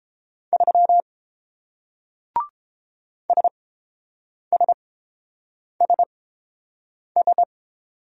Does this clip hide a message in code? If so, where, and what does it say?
Yes: Morse code from 0.5 to 7.4 s, heard as a keyed tone, about 30 wpm, spaced as for 7 wpm, 690 Hz. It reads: 3 SSSS